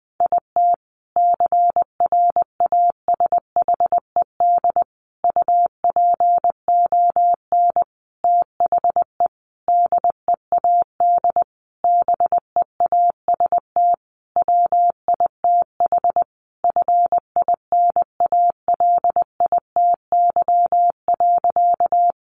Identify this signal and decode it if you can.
Morse code, heard as a keyed tone, about 20 wpm, 700 Hz. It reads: IT CRASHED UPON THE DEAD BEAST WITH FINALITY.
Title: A Sound Of Thunder 20wpm